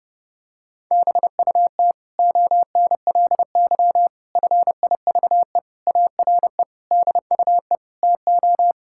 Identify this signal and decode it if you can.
Morse code, heard as a keyed tone, about 32 wpm, 700 Hz. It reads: BUT ONLY FIVE ARE DUE TO
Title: Sentences from Top 1000 Words - Review Part 6 - 30wpm